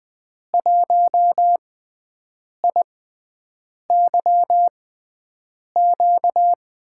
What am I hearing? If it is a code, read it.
Morse code, heard as a keyed tone, about 20 wpm, spaced as for 7 wpm, 700 Hz. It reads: 1IYQ
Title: USA Call Signs - 5 Characters - 6x Character Spacing - 20wpm